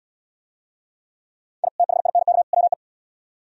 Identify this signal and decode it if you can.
Morse code, heard as a keyed tone, about 76 wpm, 700 Hz. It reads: I THINK WE